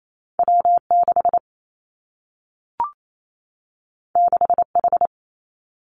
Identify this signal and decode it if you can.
Morse code, heard as a keyed tone, about 28 wpm, 700 Hz. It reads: W 6 6 H
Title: Two Challenging Letter-Number Combinations 28wpm